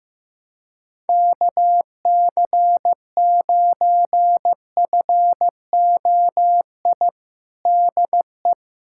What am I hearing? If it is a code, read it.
Morse code, heard as a keyed tone, about 14 wpm, 700 Hz. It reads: KC9FOI DE
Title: USA Callsign DE Callsign - 6 Characters - 15wpm